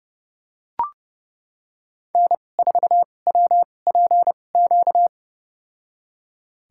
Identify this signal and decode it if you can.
Morse code, heard as a keyed tone, about 31 wpm, spaced as for 22 wpm, 700 Hz. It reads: N4WPQ
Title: USA Call Signs - 5 Characters - 2x Character Spacing - 30wpm